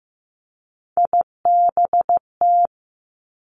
Morse code, heard as a keyed tone, about 15 wpm, 700 Hz. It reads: IBT
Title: Three Challenging Letter-Number Combinations 15wpm